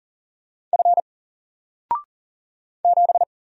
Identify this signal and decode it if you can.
Morse code, heard as a keyed tone, about 40 wpm, 700 Hz. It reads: F 7
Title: Single Letter-Number 40wpm